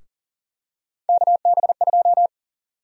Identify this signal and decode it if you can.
Morse code, heard as a keyed tone, about 37 wpm, 700 Hz. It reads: KB2
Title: USA Call Sign Prefixes - Repeated 2x - 40wpm